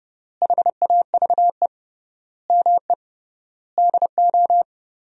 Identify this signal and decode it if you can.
Morse code, heard as a keyed tone, about 32 wpm, 700 Hz. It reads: HAVE ME DO
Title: 3x Word-Spacing - Sentences from Top 300 Words - 30wpm